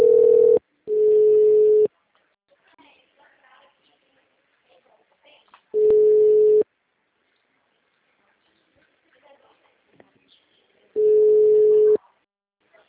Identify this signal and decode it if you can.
Ringback tone (Europe, 425 Hz), with speech.